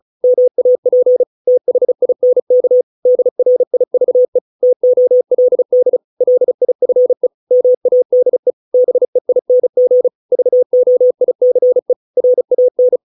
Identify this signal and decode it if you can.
Morse code, heard as a keyed tone, about 35 wpm, 500 Hz. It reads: MAP THINK DRIVE TOLD LIFE MADE BEING VOICE RAN